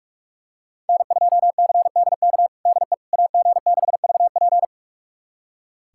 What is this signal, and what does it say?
Morse code, heard as a keyed tone, about 47 wpm, 700 Hz. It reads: N2CDK DE AG6VP
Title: USA Callsign DE Callsign - 5 Characters - 45wpm